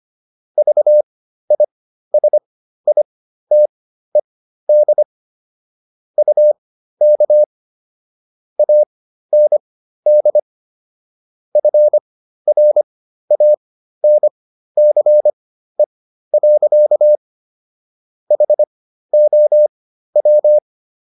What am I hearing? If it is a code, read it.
Morse code, heard as a keyed tone, about 25 wpm, spaced as for 13 wpm, 600 Hz. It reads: VISITED UK AND FRANCE. HOW